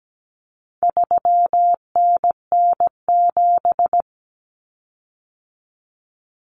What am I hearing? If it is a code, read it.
Morse code, heard as a keyed tone, about 17 wpm, 700 Hz. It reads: 3NN7
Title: Common Cut Numbers - 4-Digit 17wpm